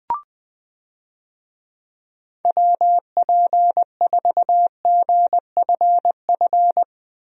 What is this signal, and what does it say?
Morse code, heard as a keyed tone, about 21 wpm, 700 Hz. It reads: WP4GFF
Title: USA Call Signs - 6 Characters - Repeated 3x - 20wpm